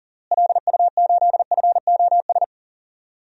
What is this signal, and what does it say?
Morse code, heard as a keyed tone, about 37 wpm, 700 Hz. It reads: LU8FOS
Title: DXCC-100 Call Signs - 6 Characters - Repeated 3x - 40wpm